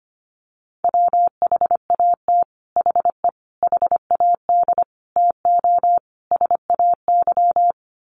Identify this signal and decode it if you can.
Morse code, heard as a keyed tone, about 25 wpm, 700 Hz. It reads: WHAT HE HAD TO SAY
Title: Sets of 5 Words Encore2 25wpm